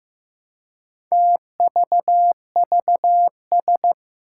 Morse code, heard as a keyed tone, about 15 wpm, 700 Hz. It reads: TVVS
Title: Four Challenging Letter-Number Combinations 15wpm